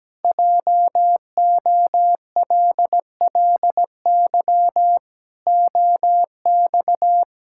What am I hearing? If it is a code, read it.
Morse code, heard as a keyed tone, about 17 wpm, 700 Hz. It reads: JOLLY OX
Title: Pangram Challenge v2 - 17wpm